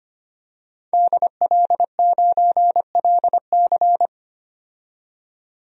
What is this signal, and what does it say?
Morse code, heard as a keyed tone, about 25 wpm, 700 Hz. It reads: DL9LC
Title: DXCC-100 Call Signs - 5 Characters - 25wpm